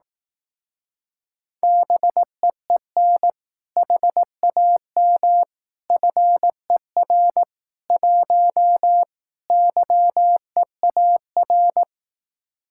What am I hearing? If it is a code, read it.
Morse code, heard as a keyed tone, about 18 wpm, 700 Hz. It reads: BEEN HAM FER 1 YEAR
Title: Speed-Racing Common QSO Phrases Encore - 15wpm